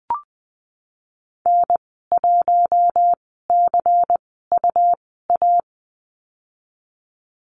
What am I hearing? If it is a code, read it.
Morse code, heard as a keyed tone, about 20 wpm, spaced as for 15 wpm, 700 Hz. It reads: N1CUA
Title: 128 Lesson 21g - Number 6 - Call Signs 20wpm